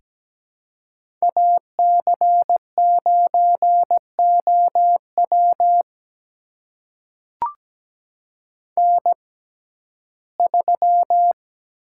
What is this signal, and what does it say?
Morse code, heard as a keyed tone, about 17 wpm, spaced as for 12 wpm, 700 Hz. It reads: AC9OW N3
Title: USA Call Signs - 6x Character Spacing - 17wpm